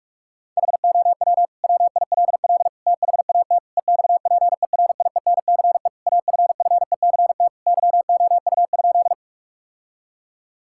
Morse code, heard as a keyed tone, about 45 wpm, 700 Hz. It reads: HOW WILL THAT EXPERIENCE AFFECT YOU?
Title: Sentences from Top 1000 Words - Review Part 6 - 45wpm